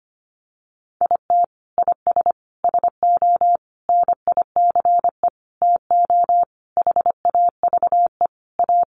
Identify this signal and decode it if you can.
Morse code, heard as a keyed tone, about 25 wpm, 700 Hz. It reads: IT IS SO NICE TO HAVE A
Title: Sentences from Top 1100 Words 25wpm